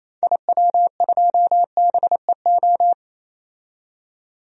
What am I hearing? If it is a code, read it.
Morse code, heard as a keyed tone, about 28 wpm, 700 Hz. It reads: IW2BEO
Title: DXCC-100 Call Signs - 6 Characters - 28wpm